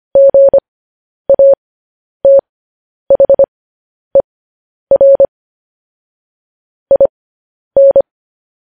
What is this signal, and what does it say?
Morse code, heard as a keyed tone, about 25 wpm, spaced as for 10 wpm, 550 Hz. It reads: GATHER IN
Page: news headlines in morse code at 10 wpm